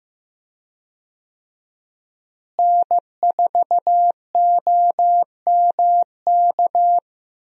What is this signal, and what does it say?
Morse code, heard as a keyed tone, about 15 wpm, 700 Hz. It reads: N4OMK
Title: USA Call Signs - 5 Characters - Repeated 3x - 15wpm